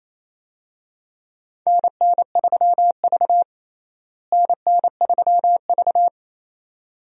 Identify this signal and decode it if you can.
Morse code, heard as a keyed tone, about 28 wpm, 700 Hz. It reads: NN3V NN3V
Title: USA Call Signs - 4 Characters - Repeated 2x - 28wpm